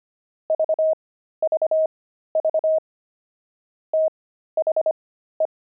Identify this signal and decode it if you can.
Morse code, heard as a keyed tone, about 25 wpm, spaced as for 13 wpm, 630 Hz. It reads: VVV THE
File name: pangram-fs-13-cs-25-fox